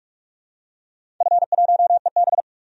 Morse code, heard as a keyed tone, about 42 wpm, 700 Hz. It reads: F1EB
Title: DXCC-100 Call Signs - 4 Characters - 45wpm